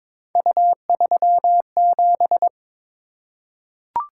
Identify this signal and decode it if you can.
Morse code, heard as a keyed tone, about 22 wpm, 700 Hz. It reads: U37
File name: practice / 22wpm / Three Letter-Number Combinations - 22wpm